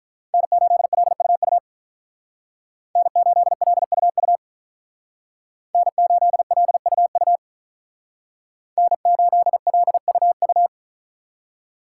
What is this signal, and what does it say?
Morse code, heard as a keyed tone, about 45 wpm, 700 Hz. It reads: N8LUU N8LUU N8LUU N8LUU